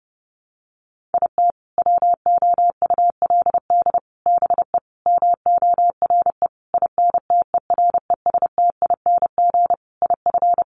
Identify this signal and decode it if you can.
Morse code, heard as a keyed tone, about 30 wpm, 700 Hz. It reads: IT WOULD BE MORE INTERESTING IF
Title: Sentences from Top 1000 Words 30wpm